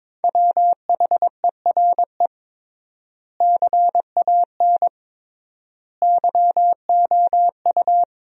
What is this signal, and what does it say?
Morse code, heard as a keyed tone, about 21 wpm, 700 Hz. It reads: WHERE CAN YOU
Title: 3x Word-Spacing - Sentences from Top 500 Words - 22wpm